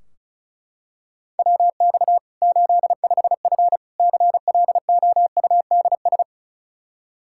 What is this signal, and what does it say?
Morse code, heard as a keyed tone, about 34 wpm, 700 Hz. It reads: WX 85F CLOUDS
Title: Practice QSO - Exchange 2 - Info - 35wpm